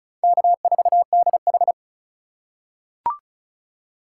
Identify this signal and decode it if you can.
Morse code, heard as a keyed tone, about 36 wpm, 700 Hz. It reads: K4DH
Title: USA Call Signs - 4 Characters - 3x Character Spacing - 35wpm